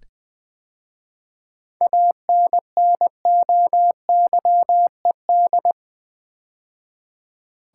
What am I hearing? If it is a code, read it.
Morse code, heard as a keyed tone, about 20 wpm, 700 Hz. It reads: ANNOYED
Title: 7 Letter Words - First 3 Letters Given - 20wpm